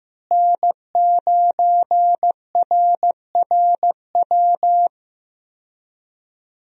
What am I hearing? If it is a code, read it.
Morse code, heard as a keyed tone, about 15 wpm, 700 Hz. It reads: N9RRW
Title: USA Call Signs - 5 Characters - Repeated 2x - 15wpm